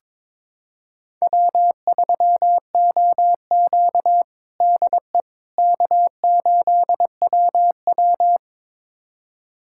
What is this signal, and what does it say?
Morse code, heard as a keyed tone, about 21 wpm, 700 Hz. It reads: W3OQ DE K8WW
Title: USA Callsign DE Callsign - 4 Characters - 22wpm